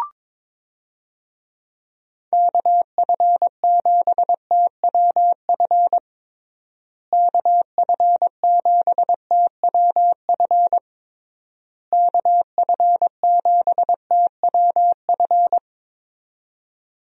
Morse code, heard as a keyed tone, about 22 wpm, 700 Hz. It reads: KF7TWF KF7TWF KF7TWF
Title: USA Call Signs - 6 Characters - Repeated 3x - 22wpm